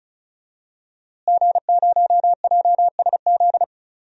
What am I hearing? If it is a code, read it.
Morse code, heard as a keyed tone, about 36 wpm, 700 Hz. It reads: G0JSZ
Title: DXCC-100 Call Signs - 5 Characters - Repeated 2x - 35wpm